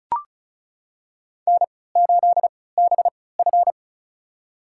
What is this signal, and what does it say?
Morse code, heard as a keyed tone, about 36 wpm, spaced as for 20 wpm, 700 Hz. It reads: N8BF
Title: USA Call Signs - 4 Characters - 3x Character Spacing - 35wpm